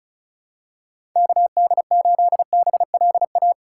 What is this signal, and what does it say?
Morse code, heard as a keyed tone, about 36 wpm, 700 Hz. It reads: KD8BLA
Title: USA Call Signs - 3x Character Spacing - 35wpm